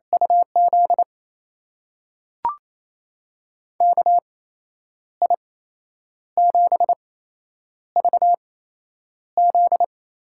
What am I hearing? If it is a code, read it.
Morse code, heard as a keyed tone, about 28 wpm, spaced as for 9 wpm, 700 Hz. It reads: UZ KI7VZ
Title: USA Call Signs - 6 Characters - 8x Character Spacing - 28wpm